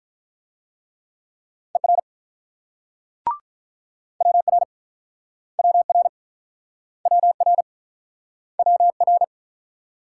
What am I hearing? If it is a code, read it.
Morse code, heard as a keyed tone, about 47 wpm, 700 Hz. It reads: ER WR WR WR WR